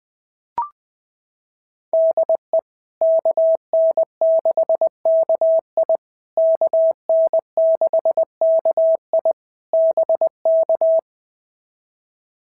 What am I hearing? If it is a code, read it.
Morse code, heard as a keyed tone, about 18 wpm, 650 Hz. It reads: DE KN6KI KN6KI BK
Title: POTA QSO Practice - 20wpm